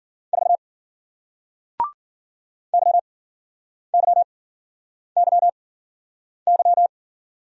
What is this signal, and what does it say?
Morse code, heard as a keyed tone, about 53 wpm, spaced as for 9 wpm, 700 Hz. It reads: 4 YYYY